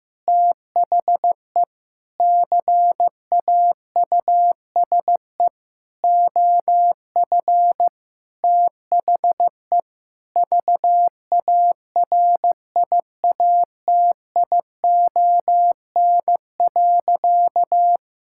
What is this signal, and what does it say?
Morse code, heard as a keyed tone, about 15 wpm, 700 Hz. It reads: THE CAUSE OF THE VARIATION.
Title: Sentences from Top 1600 Words 15wpm